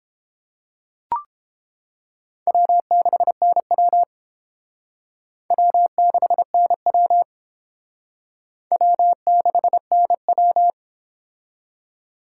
Morse code, heard as a keyed tone, about 28 wpm, 700 Hz. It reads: W6NW W6NW W6NW